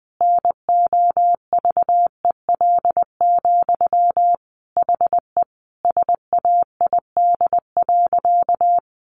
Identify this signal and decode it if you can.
Morse code, heard as a keyed tone, about 20 wpm, 700 Hz. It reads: NOVEL, HE SAID.
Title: Sentences from Top 1300 Words 20wpm